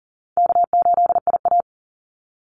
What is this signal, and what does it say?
Morse code, heard as a keyed tone, about 40 wpm, 700 Hz. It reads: K8IA
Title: Call Signs - 40wpm